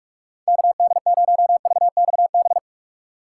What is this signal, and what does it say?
Morse code, heard as a keyed tone, about 46 wpm, 700 Hz. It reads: KD0VXB
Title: USA Call Signs - 6 Characters - 6x Character Spacing - 45wpm